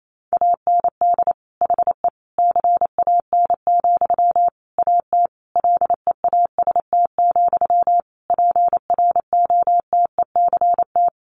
Morse code, heard as a keyed tone, about 28 wpm, 700 Hz. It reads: AND HE CAN, AT LEAST, PROTECT
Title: Sentences from Top 900 Words 28wpm